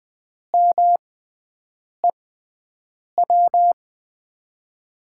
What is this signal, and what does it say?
Morse code, heard as a keyed tone, about 20 wpm, spaced as for 7 wpm, 700 Hz. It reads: MEW
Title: USA Call Signs - 6x Character Spacing - 20wpm